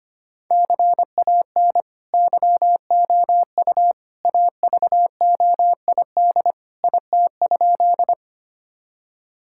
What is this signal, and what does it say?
Morse code, heard as a keyed tone, about 25 wpm, 700 Hz. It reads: CAN YOU AVOID IT?
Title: Sentences from Top 1000 Words - Review Part 6 - 25wpm